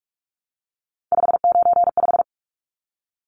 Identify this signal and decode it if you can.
Morse code, heard as a keyed tone, about 44 wpm, 700 Hz. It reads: 595
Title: Numbers - 3-Digit 45wpm